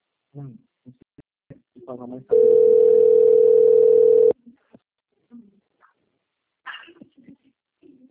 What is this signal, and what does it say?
Ringback tone (North America), 440 + 480 Hz, with speech.
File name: out- - - - - .43266